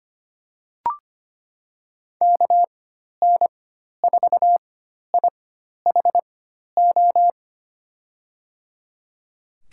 Morse code, heard as a keyed tone, about 25 wpm, spaced as for 12 wpm, 700 Hz. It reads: KN4IHO
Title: USA Call Signs - 6 Characters - 4x Character Spacing - 25wpm